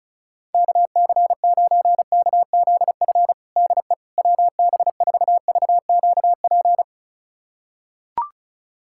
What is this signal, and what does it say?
Morse code, heard as a keyed tone, about 34 wpm, 700 Hz. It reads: KC9KZF DE WB4VQP
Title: USA Callsign DE Callsign - 6 Characters - 35wpm